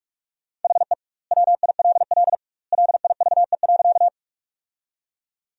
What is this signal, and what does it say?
Morse code, heard as a keyed tone, about 43 wpm, 700 Hz. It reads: HE WILL LIVE.
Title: 2x Word-Spacing - Sentences from Top 200 Words - 45wpm